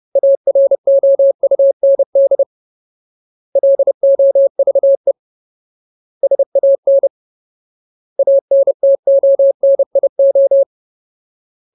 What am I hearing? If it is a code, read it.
Morse code, heard as a keyed tone, about 30 wpm, 550 Hz. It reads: AROUND LOVE SAN ANTONIO
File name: Ultimate Word List / 30WPM Ultimate